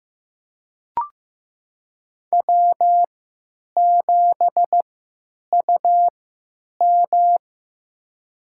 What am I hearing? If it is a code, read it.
Morse code, heard as a keyed tone, about 15 wpm, spaced as for 9 wpm, 700 Hz. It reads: W7UM
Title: USA Call Signs - 4 Characters - 3x Character Spacing - 15wpm